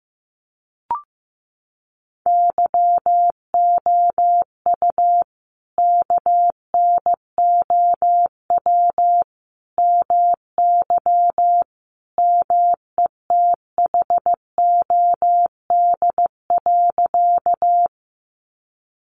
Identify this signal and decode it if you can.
Morse code, heard as a keyed tone, about 15 wpm, 700 Hz. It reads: YOU KNOW MY METHOD.
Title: Sentences from Top 600 Words 15wpm